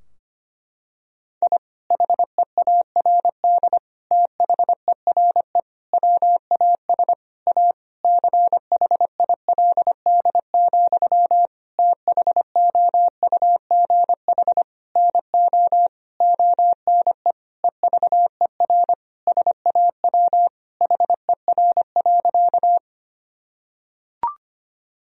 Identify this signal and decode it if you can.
Morse code, heard as a keyed tone, about 25 wpm, 700 Hz. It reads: I HEARD THERE WAS A CHILD, THOUGH NO ONE EVER SAW HER.
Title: Sentences from Top 1000 Words - Review Part 2 - 25wpm